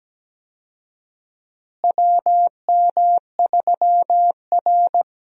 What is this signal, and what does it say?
Morse code, heard as a keyed tone, about 17 wpm, 700 Hz. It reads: WM3R
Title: USA Call Signs - 4 Characters - Repeated 3x - 17wpm